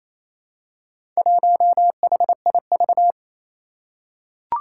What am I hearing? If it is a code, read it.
Morse code, heard as a keyed tone, about 28 wpm, 700 Hz. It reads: 1HIV
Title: Four Challenging Letter-Number Combinations 28wpm